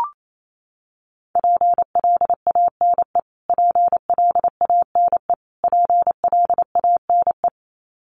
Morse code, heard as a keyed tone, about 28 wpm, 700 Hz. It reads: PLANE PLANE PLANE
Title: Top 500 Words Repeated 28wpm